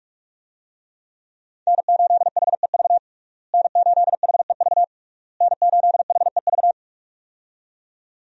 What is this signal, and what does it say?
Morse code, heard as a keyed tone, about 46 wpm, 700 Hz. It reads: N8HEV N8HEV N8HEV
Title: USA Call Signs - 5 Characters - Repeated 3x - 45wpm